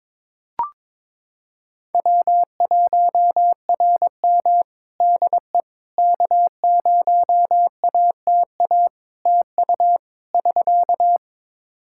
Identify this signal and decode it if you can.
Morse code, heard as a keyed tone, about 22 wpm, 700 Hz. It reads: W1RM DE K0ATA TU <SK>
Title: Practice QSO - Exchange 4 - Ending - 22wpm